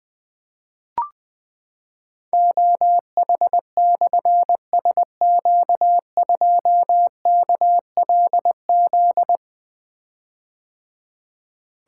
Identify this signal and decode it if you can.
Morse code, heard as a keyed tone, about 21 wpm, 700 Hz. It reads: OH/SQ2KLZ
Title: Difficult Call Signs 20wpm